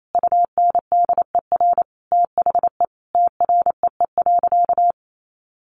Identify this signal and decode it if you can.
Morse code, heard as a keyed tone, about 28 wpm, 700 Hz. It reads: UNDER THE TREE.
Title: Sentences from Top 1400 Words 28wpm